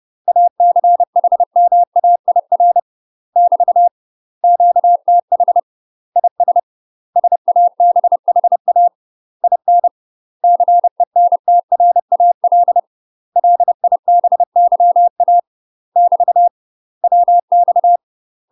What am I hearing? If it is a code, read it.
Morse code, heard as a keyed tone, about 30 wpm, 700 Hz. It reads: ACHMAIR <BT> QTH IS SABHA IN CENTRAL LIBYA <BT> WX
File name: UK MASTER QSOs TRAINING/30WPM